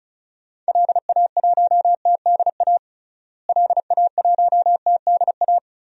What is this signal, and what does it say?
Morse code, heard as a keyed tone, about 36 wpm, 700 Hz. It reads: LA1TDA LA1TDA
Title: DXCC-100 Call Signs - 6 Characters - Repeated 2x - 35wpm